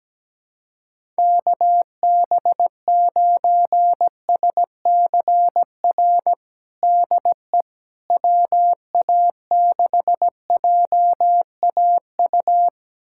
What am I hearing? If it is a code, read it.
Morse code, heard as a keyed tone, about 17 wpm, 700 Hz. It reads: KB9SCR DE WA6JAU
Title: USA Callsign DE Callsign - 6 Characters - 17wpm